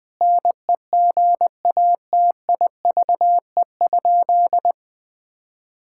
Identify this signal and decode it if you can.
Morse code, heard as a keyed tone, about 20 wpm, 700 Hz. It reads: NEGATIVE?
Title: Sentences from Top 1000 Words - Review Part 6 - 20wpm